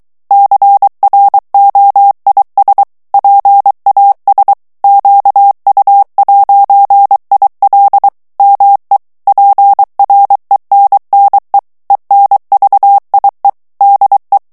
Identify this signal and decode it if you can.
Morse code, heard as a keyed tone, about 23 wpm, 800 Hz. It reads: CROIS PAS QU'IL ME PRENNE ENVIE DE
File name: morse mp3 / FRANCAIS TEXTE CLAIR 2 VITESSE 22 MOTS MINUTES